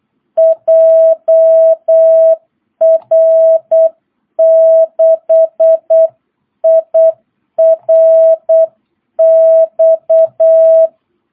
Morse code, heard as a keyed tone, about 8 wpm, 660 Hz. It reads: JR6IRX
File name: jh8cbh / 2022cw